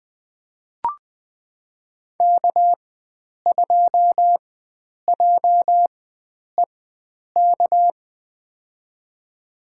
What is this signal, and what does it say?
Morse code, heard as a keyed tone, about 20 wpm, spaced as for 9 wpm, 700 Hz. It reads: K2JEK
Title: USA Call Signs - 5 Characters - 4x Character Spacing - 20wpm